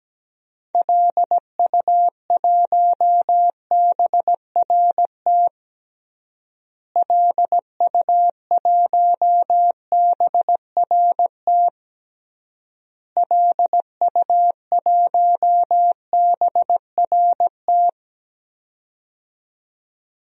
Morse code, heard as a keyed tone, about 17 wpm, 700 Hz. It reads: LU1BRT LU1BRT LU1BRT
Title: DXCC-100 Call Signs - 6 Characters - Repeated 3x - 17wpm